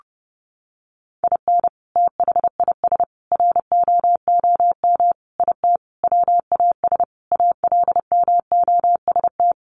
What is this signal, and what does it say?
Morse code, heard as a keyed tone, about 30 wpm, 700 Hz. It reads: IN THIS ROOM IT WAS ALMOST
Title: Sentences from Top 700 Words 30wpm